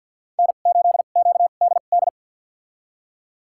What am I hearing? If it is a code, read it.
Morse code, heard as a keyed tone, about 47 wpm, spaced as for 35 wpm, 690 Hz. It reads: N8QDD
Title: USA Call Signs - 2x Character Spacing - 50wpm